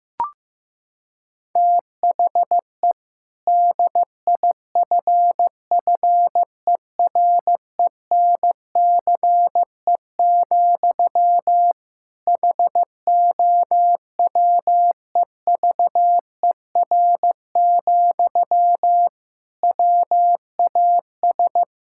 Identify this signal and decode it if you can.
Morse code, heard as a keyed tone, about 15 wpm, 700 Hz. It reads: THE DIFFERENCE, HOWEVER, WAS
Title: Sentences from Top 1000 Words - Review Part 5 - 15wpm